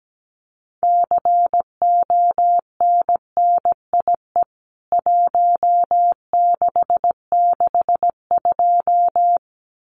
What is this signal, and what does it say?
Morse code, heard as a keyed tone, about 17 wpm, 700 Hz. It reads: CONNIE 1662
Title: CWT Exchange 17wpm